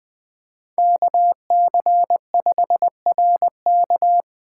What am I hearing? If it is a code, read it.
Morse code, heard as a keyed tone, about 20 wpm, 700 Hz. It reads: KC5RK
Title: USA Call Signs - 5 Characters - 2x Character Spacing - 20wpm